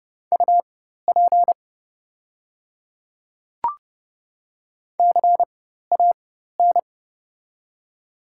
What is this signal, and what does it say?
Morse code, heard as a keyed tone, about 30 wpm, spaced as for 14 wpm, 700 Hz. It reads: UP CAN
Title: 4x Character Spacing - Top 100 Words - 30wpm